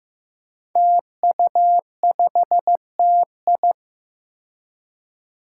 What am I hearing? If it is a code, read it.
Morse code, heard as a keyed tone, about 15 wpm, 700 Hz. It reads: TU5TI
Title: Five Challenging Letter-Number Combinations 15wpm